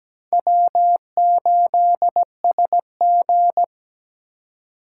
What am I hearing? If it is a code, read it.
Morse code, heard as a keyed tone, about 20 wpm, 700 Hz. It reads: W8SG
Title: Speed Racing - USA 1x2 Call Signs 17wpm